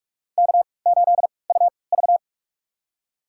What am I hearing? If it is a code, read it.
Morse code, heard as a keyed tone, about 44 wpm, spaced as for 25 wpm, 690 Hz. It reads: K8UV
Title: USA Call Signs - 4 Characters - 3x Character Spacing - 45wpm